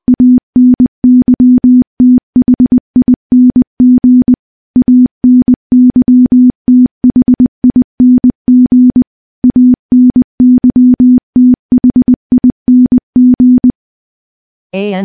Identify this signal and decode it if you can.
Morse code, heard as a keyed tone, about 20 wpm, 250 Hz. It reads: ANYTHING ANYTHING ANYTHING
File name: morse-training-20WPM